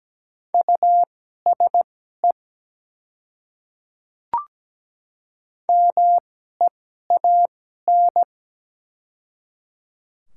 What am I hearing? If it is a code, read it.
Morse code, heard as a keyed tone, about 17 wpm, spaced as for 12 wpm, 700 Hz. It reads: USE MEAN
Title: 2x Character Spacing - Top 200 Words - 17wpm